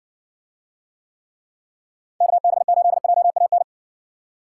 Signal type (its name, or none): Morse code